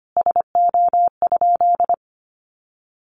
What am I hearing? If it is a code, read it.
Morse code, heard as a keyed tone, about 25 wpm, 700 Hz. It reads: SO?
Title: Sentences from Top 1400 Words 25wpm